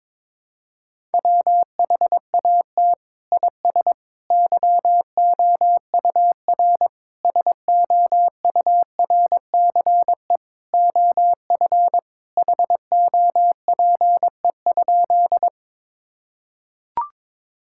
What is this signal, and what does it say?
Morse code, heard as a keyed tone, about 22 wpm, 700 Hz. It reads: WHAT IS YOUR SOURCE OF HOPE?
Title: Sentences from Top 1000 Words - Review Part 5 - 22wpm